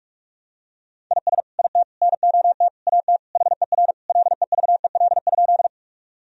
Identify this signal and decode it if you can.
Morse code, heard as a keyed tone, about 46 wpm, 700 Hz. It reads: IS IT NOT AT HER LEVEL?
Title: Sentences from Top 1000 Words - Review Part 4 - 45wpm